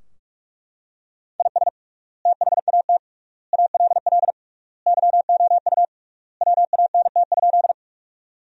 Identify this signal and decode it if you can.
Morse code, heard as a keyed tone, about 43 wpm, 700 Hz. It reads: IS THAT ALL YOU WANT?
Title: 3x Word-Spacing - Sentences from Top 200 Words - 45wpm